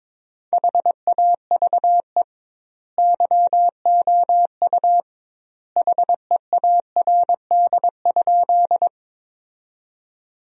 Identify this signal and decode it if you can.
Morse code, heard as a keyed tone, about 22 wpm, 700 Hz. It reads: HAVE YOU HEARD?
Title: 2x Word-Spacing - Sentences from Top 400 Words - 22wpm